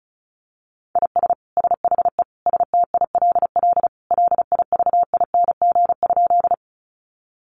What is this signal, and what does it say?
Morse code, heard as a keyed tone, about 34 wpm, 700 Hz. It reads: IS SHE STILL LIVING?
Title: Sentences from Top 1100 Words 35wpm